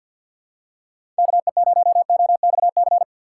Morse code, heard as a keyed tone, about 51 wpm, 700 Hz. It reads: KE0OXC
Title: USA Call Signs - 3x Character Spacing - 50wpm